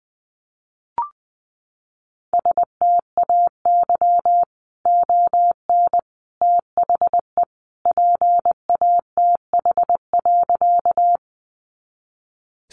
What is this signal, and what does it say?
Morse code, heard as a keyed tone, about 20 wpm, 700 Hz. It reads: STAY ON THE PATH.
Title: A Sound Of Thunder 20wpm